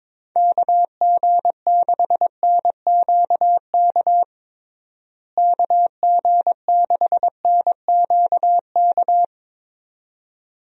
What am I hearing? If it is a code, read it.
Morse code, heard as a keyed tone, about 21 wpm, 700 Hz. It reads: KG6NQK KG6NQK
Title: USA Call Signs - 6 Characters - Repeated 2x - 22wpm